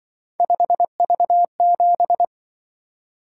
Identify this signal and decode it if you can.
Morse code, heard as a keyed tone, about 25 wpm, 700 Hz. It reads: 5V7
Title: Speed-Racing - Three Letter-Number Combinations - 20wpm